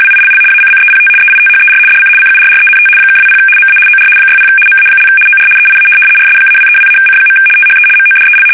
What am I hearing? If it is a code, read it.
RTTY, about 75 baud. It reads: WILL ACCEPT EITHER DC CURRENT (20 MA OR 60 MA) OR EIA RS-232-C. THE MODEL
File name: fsk850-75B / teletype-corp